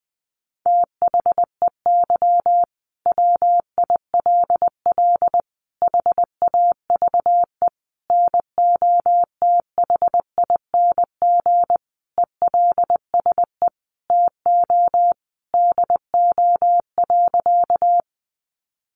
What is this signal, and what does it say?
Morse code, heard as a keyed tone, about 20 wpm, 700 Hz. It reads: THEY WILL HAVE NOTHING ELSE TO DO.
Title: Sentences from Top 800 Words 20wpm